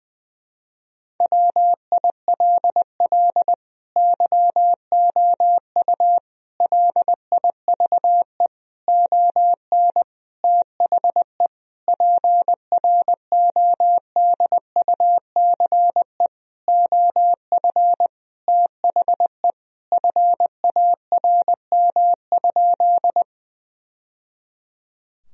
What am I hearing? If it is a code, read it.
Morse code, heard as a keyed tone, about 20 wpm, 700 Hz. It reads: WILL YOU LIVE ON THE PRODUCE OF THE FARM?
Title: Sentences from Top 1000 Words - Review Part 3 - 20wpm